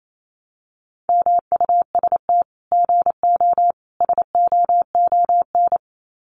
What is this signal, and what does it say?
Morse code, heard as a keyed tone, about 28 wpm, 700 Hz. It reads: MUST GO SOON
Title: Common QSO Phrases 28wpm